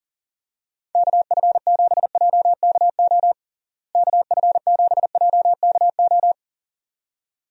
Morse code, heard as a keyed tone, about 37 wpm, 700 Hz. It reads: KF7JKO KF7JKO
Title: USA Call Signs - 6 Characters - Repeated 2x - 40wpm